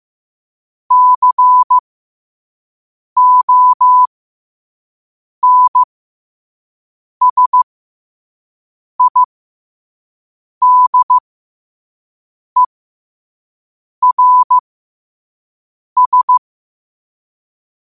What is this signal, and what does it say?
Morse code, heard as a keyed tone, about 15 wpm, spaced as for 5 wpm, 1000 Hz. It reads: CONSIDERS
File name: qotdfiles / 5wpm